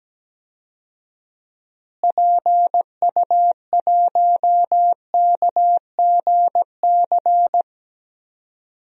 Morse code, heard as a keyed tone, about 17 wpm, 700 Hz. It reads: PU1KGC
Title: DXCC-100 Call Signs - 6 Characters - Repeated 3x - 17wpm